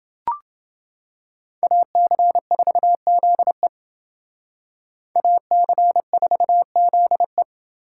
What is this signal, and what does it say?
Morse code, heard as a keyed tone, about 28 wpm, 700 Hz. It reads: AC4ZE AC4ZE